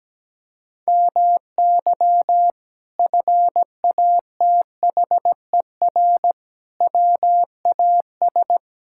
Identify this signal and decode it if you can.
Morse code, heard as a keyed tone, about 17 wpm, 700 Hz. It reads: MY FATHER WAS
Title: Sentences from Top 1000 Words - Review Part 6 - 17wpm